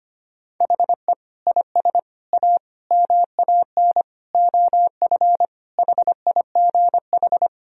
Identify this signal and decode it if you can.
Morse code, heard as a keyed tone, about 25 wpm, 700 Hz. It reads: HE IS A MAN OF HIGH
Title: Sentences from Top 1000 Words - Review Part 2 - 25wpm